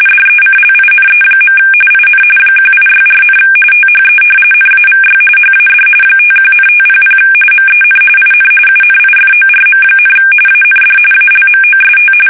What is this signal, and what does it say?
RTTY, about 45.45 baud. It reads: DECLINE. IN 1864, THE NAVY HAD 51,500 MEN IN UNIFORM, AND ALMOST
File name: fsk850 / usnavy3